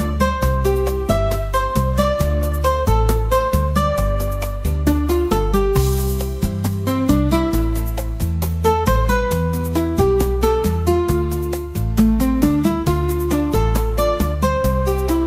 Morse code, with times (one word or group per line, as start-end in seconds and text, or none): none